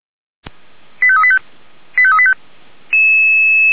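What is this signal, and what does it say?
Five-tone selective call (ZVEI), heard as quick runs of five tones, tones of about 70 ms. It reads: 85385 85385
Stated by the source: ZVEI five-tone selective call 85371, SDR recording.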